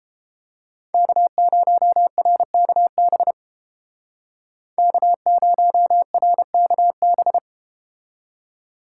Morse code, heard as a keyed tone, about 32 wpm, 700 Hz. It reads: K0RKB K0RKB